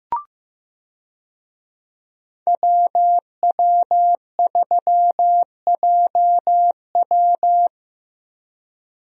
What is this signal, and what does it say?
Morse code, heard as a keyed tone, about 15 wpm, 700 Hz. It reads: WW3JW
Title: USA Call Signs - 5 Characters - Repeated 3x - 15wpm